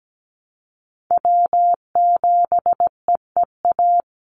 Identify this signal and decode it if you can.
Morse code, heard as a keyed tone, about 17 wpm, 700 Hz. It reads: W7EEA
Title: USA Call Signs 17wpm